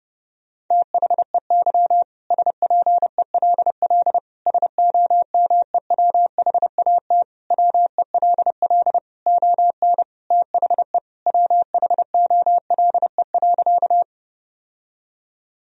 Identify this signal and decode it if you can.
Morse code, heard as a keyed tone, about 32 wpm, 700 Hz. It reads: THEY SPELL SOMEWHAT WELL ON THE WHOLE.
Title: Sentences from Top 1000 Words - Review Part 1 - 30wpm